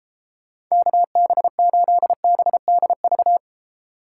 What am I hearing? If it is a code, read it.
Morse code, heard as a keyed tone, about 36 wpm, 700 Hz. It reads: KB8BDV